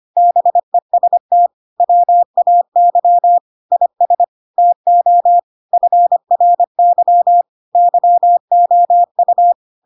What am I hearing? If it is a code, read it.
Morse code, heard as a keyed tone, about 25 wpm, 700 Hz. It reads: BEST WAY IS TO FRY YOU
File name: Pinocchio / Pinocchio ch5 25 25 ik1hge